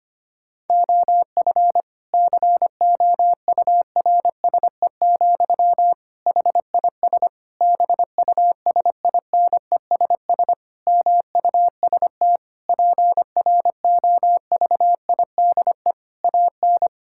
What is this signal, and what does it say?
Morse code, heard as a keyed tone, about 25 wpm, 700 Hz. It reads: OF COURSE, HIS BUSINESS MUST PROVIDE AN
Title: Sentences from Top 1000 Words - Review Part 5 - 25wpm